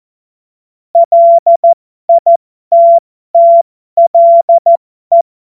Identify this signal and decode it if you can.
Morse code, heard as a keyed tone, about 14 wpm, spaced as for 12 wpm, 680 Hz. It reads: LITTLE
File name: Morsekurs / Witze